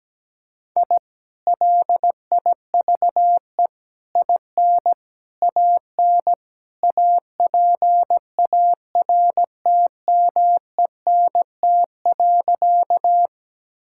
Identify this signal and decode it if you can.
Morse code, heard as a keyed tone, about 17 wpm, 700 Hz. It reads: I LIVE IN AN APARTMENT.
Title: Sentences from Top 1000 Words - Review Part 6 - 17wpm